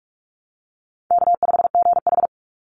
Morse code, heard as a keyed tone, about 44 wpm, 700 Hz. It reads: K5GH
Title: Call Signs Repeated 2x - 45wpm